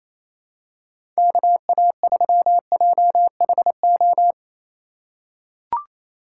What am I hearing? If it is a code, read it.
Morse code, heard as a keyed tone, about 28 wpm, 700 Hz. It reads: KA3JHO
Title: USA Call Signs - 6 Characters - 3x Character Spacing - 28wpm